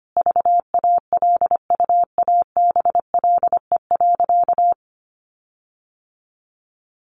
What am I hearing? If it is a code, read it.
Morse code, heard as a keyed tone, about 25 wpm, 700 Hz. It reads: VALUABLE.
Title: Sentences from Top 1400 Words 25wpm